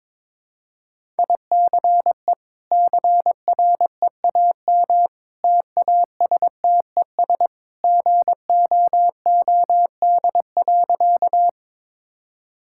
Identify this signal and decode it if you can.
Morse code, heard as a keyed tone, about 22 wpm, 700 Hz. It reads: ICE CREAM TASTES GOOD.
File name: practice / 22wpm / Easy Sentences with Hints - 22wpm